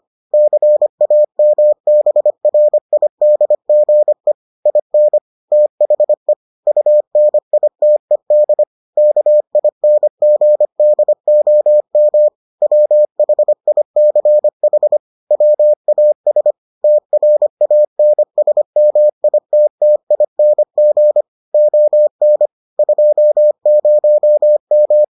Morse code, heard as a keyed tone, about 25 wpm, 600 Hz. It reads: CAMBRIDGE IN THE UNITED KINGDOM WHICH WAS TRANSMITTING ON 20M